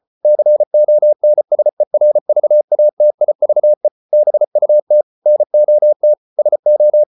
Morse code, heard as a keyed tone, about 34 wpm, 600 Hz. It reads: CONSERVATIVE BUT NOT SO